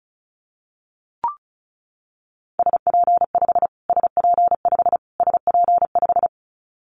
Morse code, heard as a keyed tone, about 34 wpm, 700 Hz. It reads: SP5 SP5 SP5
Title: Callsign Prefixes for Top 25 DXCC - Repeated 3x - 35wpm